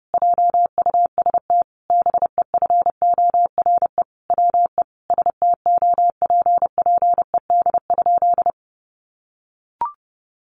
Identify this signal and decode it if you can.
Morse code, heard as a keyed tone, about 30 wpm, 700 Hz. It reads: JUST BEFORE WE STOPPED?
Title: A Sound Of Thunder 30wpm